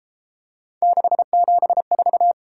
Morse code, heard as a keyed tone, about 31 wpm, 700 Hz. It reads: 674